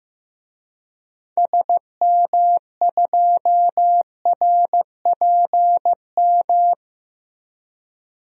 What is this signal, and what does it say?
Morse code, heard as a keyed tone, about 15 wpm, 700 Hz. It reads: SM2RPM
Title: DXCC-100 Call Signs - 6 Characters - Repeated 3x - 15wpm